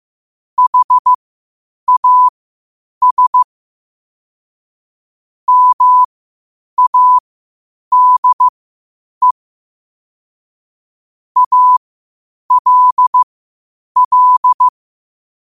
Morse code, heard as a keyed tone, about 15 wpm, spaced as for 8 wpm, 1000 Hz. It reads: HAS MADE ALL
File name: qotdfiles / 8wpm